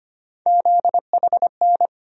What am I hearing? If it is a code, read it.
Morse code, heard as a keyed tone, about 25 wpm, 700 Hz. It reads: ZHN
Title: USA Call Sign Postfixes - 3 Letters 25wpm